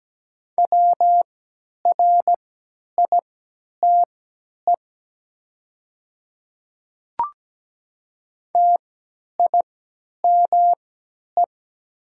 Morse code, heard as a keyed tone, about 17 wpm, spaced as for 10 wpm, 700 Hz. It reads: WRITE TIME
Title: 3x Character Spacing - Top 200 Words - 17wpm